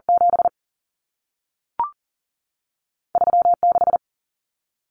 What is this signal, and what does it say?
Morse code, heard as a keyed tone, about 40 wpm, spaced as for 3 wpm, 700 Hz. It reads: 73 6
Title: Common Cut Numbers - 2-Digit 40wpm